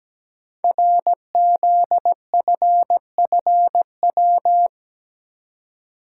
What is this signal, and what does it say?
Morse code, heard as a keyed tone, about 17 wpm, 700 Hz. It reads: RZFFW
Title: Code Groups - 5 Letters - 17wpm